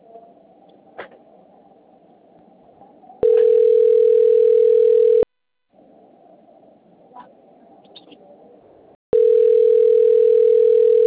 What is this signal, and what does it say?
Ringback tone (North America), 440 + 480 Hz.